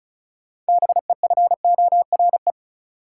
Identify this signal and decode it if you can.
Morse code, heard as a keyed tone, about 34 wpm, 700 Hz. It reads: BEFORE
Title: Six Letter Words - 35wpm